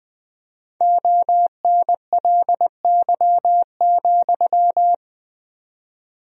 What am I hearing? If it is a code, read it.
Morse code, heard as a keyed tone, about 21 wpm, 700 Hz. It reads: ONLY,
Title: 4x Word-Spacing - Sentences from Top 300 Words - 20wpm